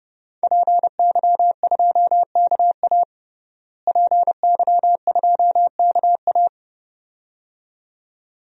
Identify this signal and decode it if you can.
Morse code, heard as a keyed tone, about 30 wpm, 700 Hz. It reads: PY2KA PY2KA
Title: DXCC-100 Call Signs - 5 Characters - Repeated 2x - 30wpm